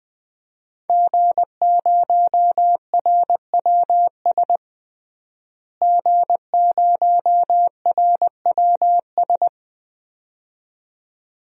Morse code, heard as a keyed tone, about 20 wpm, 700 Hz. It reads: G0RWS G0RWS
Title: DXCC-100 Call Signs - 5 Characters - Repeated 2x - 20wpm